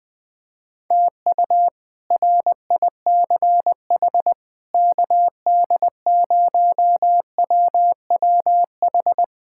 Morse code, heard as a keyed tone, about 20 wpm, 700 Hz. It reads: TU RICH KD0WWH
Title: K1USN SST QSO Practice - 20wpm